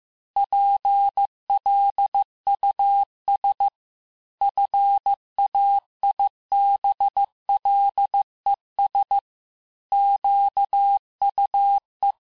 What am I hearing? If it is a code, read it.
Morse code, heard as a keyed tone, about 15 wpm, 800 Hz. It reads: PLUS FAIBLES QUE
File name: cours-du-lundi files / cours-1585-texte-FIN-de-QSO-vitesse-14mpm